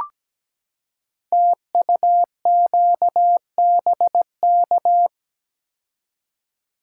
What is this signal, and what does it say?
Morse code, heard as a keyed tone, about 17 wpm, 700 Hz. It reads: TUQBK
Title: Code Groups - 5 Letters - 17wpm